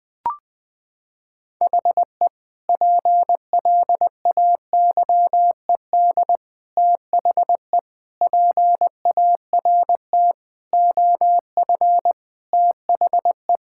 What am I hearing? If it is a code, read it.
Morse code, heard as a keyed tone, about 20 wpm, 700 Hz. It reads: HE PLAYED THE PART OF THE
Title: Sentences with 4-Word Phrases - 20wpm